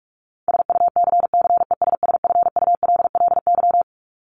Morse code, heard as a keyed tone, about 44 wpm, 700 Hz. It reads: SUCCESSFULLY
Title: LY Suffix - Top 100 Words with Base Words 45wpm